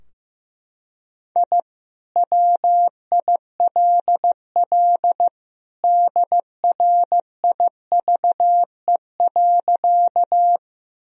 Morse code, heard as a keyed tone, about 15 wpm, 700 Hz. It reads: I WILL DRIVE.
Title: Sentences from Top 1000 Words - Review Part 2 - 15wpm